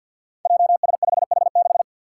Morse code, heard as a keyed tone, about 50 wpm, 700 Hz. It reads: JS5H6
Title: Five Challenging Letter-Number Combinations 50wpm